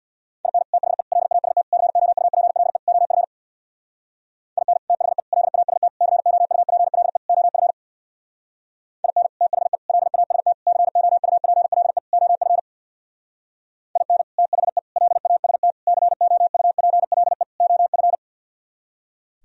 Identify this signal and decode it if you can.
Morse code, heard as a keyed tone, about 60 wpm, 700 Hz. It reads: IN THE LAST COUPLE OF IN THE LAST COUPLE OF IN THE LAST COUPLE OF IN THE LAST COUPLE OF
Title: Speed-Racing - Sets of 5 Words 50wpm